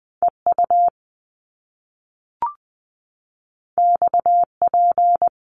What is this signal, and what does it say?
Morse code, heard as a keyed tone, about 20 wpm, 700 Hz. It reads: EU XP
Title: Top Two Letter Combinations in Words 20wpm